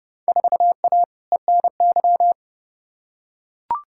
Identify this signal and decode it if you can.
Morse code, heard as a keyed tone, about 30 wpm, 700 Hz. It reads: 4A ENY
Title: Field Day Exchanges Encore - 30wpm